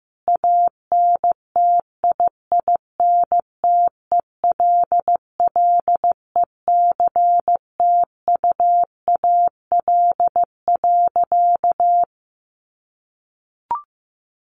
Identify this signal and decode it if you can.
Morse code, heard as a keyed tone, about 15 wpm, 700 Hz. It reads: ANTIINTELLECTUAL.
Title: A Sound Of Thunder 15wpm